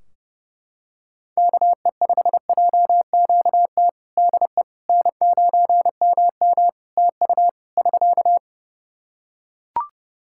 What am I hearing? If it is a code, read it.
Morse code, heard as a keyed tone, about 30 wpm, 700 Hz. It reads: KE5JQT DE N9MM TU <SK>
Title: K1USN SST QSO Practice 30wpm